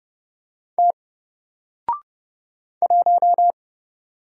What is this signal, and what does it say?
Morse code, heard as a keyed tone, about 29 wpm, 700 Hz. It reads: T 1
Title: Single Letter-Number Alt 30wpm